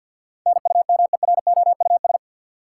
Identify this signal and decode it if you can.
Morse code, heard as a keyed tone, about 51 wpm, 700 Hz. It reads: NUMEROUS
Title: Sentences from Top 1400 Words - New Word Drill Alt - 50wpm